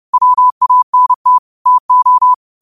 Morse code, heard as a keyed tone, about 30 wpm, 1000 Hz. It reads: WANT TO
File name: qotdfiles / 30wpm